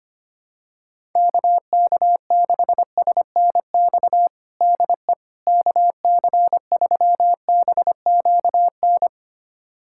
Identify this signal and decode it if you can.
Morse code, heard as a keyed tone, about 25 wpm, 700 Hz. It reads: KK6SNX DE KC3BQN
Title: USA Callsign DE Callsign - 6 Characters - 25wpm